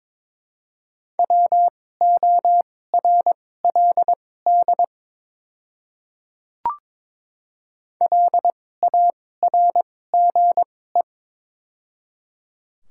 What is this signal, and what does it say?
Morse code, heard as a keyed tone, about 23 wpm, spaced as for 16 wpm, 700 Hz. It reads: WORLD LARGE
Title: 2x Character Spacing - Top 300 Words - 22wpm